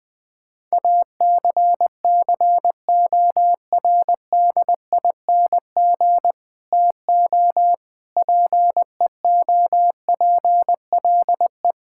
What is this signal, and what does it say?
Morse code, heard as a keyed tone, about 20 wpm, 700 Hz. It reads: ACCORDING TO PEOPLE
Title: Sentences with 7-Word Phrases - 20wpm